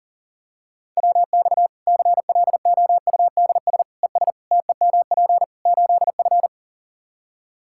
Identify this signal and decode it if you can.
Morse code, heard as a keyed tone, about 40 wpm, 700 Hz. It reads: WX CLOUDS ES TEMP 8F
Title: Practice QSO - Exchange 2 - Info - 40wpm